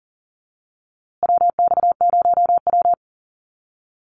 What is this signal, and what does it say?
Morse code, heard as a keyed tone, about 40 wpm, 700 Hz. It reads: WX0W
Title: USA Call Signs Repeated 2x 40wpm